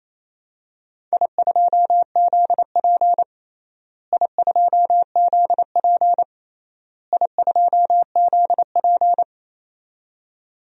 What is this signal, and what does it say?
Morse code, heard as a keyed tone, about 28 wpm, 700 Hz. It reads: I2ZP I2ZP I2ZP
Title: DXCC-100 Call Signs - 4 Characters - Repeated 3x - 28wpm